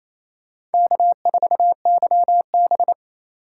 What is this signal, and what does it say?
Morse code, heard as a keyed tone, about 28 wpm, 700 Hz. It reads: K4YB
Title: USA Call Signs - 4 Characters - 4x Character Spacing - 28wpm